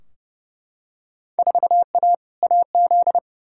Morse code, heard as a keyed tone, about 30 wpm, 700 Hz. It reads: 4A AZ
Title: Field Day Exchanges Encore - 30wpm